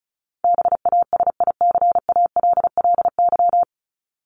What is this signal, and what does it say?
Morse code, heard as a keyed tone, about 34 wpm, 700 Hz. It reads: BASICALLY
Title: Top 1251-1500 Words 35wpm